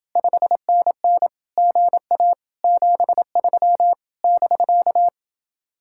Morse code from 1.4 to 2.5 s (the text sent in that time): GA